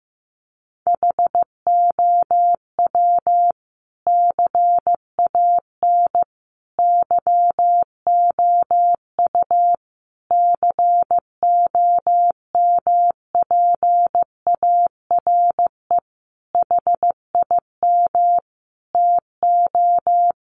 Morse code, heard as a keyed tone, about 15 wpm, 700 Hz. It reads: HOW CAN YOU COMPARE HIM TO
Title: Sentences from Top 1300 Words 15wpm